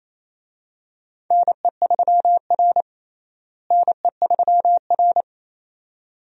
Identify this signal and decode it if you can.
Morse code, heard as a keyed tone, about 28 wpm, 700 Hz. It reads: NE3R NE3R
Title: USA Call Signs - 4 Characters - Repeated 2x - 28wpm